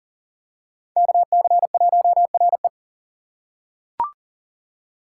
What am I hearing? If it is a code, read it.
Morse code, heard as a keyed tone, about 37 wpm, 700 Hz. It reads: KC1RE
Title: USA Call Signs - 5 Characters - 2x Character Spacing - 40wpm